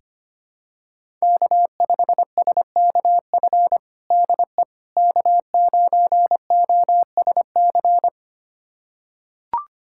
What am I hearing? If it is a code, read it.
Morse code, heard as a keyed tone, about 25 wpm, 700 Hz. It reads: K5SKF DE K9OSC
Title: USA Callsign DE Callsign - 5 Characters - 25wpm